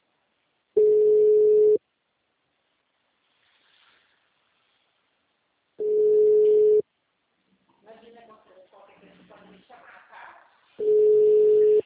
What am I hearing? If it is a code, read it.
Ringback tone (Europe, 425 Hz), with speech.